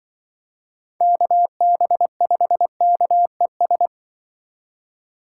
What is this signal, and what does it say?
Morse code, heard as a keyed tone, about 24 wpm, 700 Hz. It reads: KB5KES
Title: Speed Racing - USA 2x3 Call Signs 20wpm